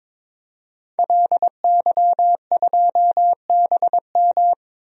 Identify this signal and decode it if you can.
Morse code, heard as a keyed tone, about 22 wpm, 700 Hz. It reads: LY2BM
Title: DXCC-100 Call Signs - 5 Characters - 22wpm